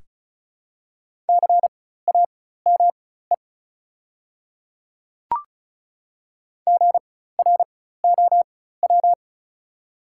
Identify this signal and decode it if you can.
Morse code, heard as a keyed tone, about 36 wpm, spaced as for 16 wpm, 690 Hz. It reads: CAME GROW
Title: 4x Character Spacing - Top 300 Words - 35wpm